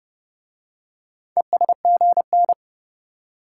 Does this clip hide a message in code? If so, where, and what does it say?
Yes: Morse code from 1.4 to 2.5 s, heard as a keyed tone, about 29 wpm, 700 Hz. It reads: ESGN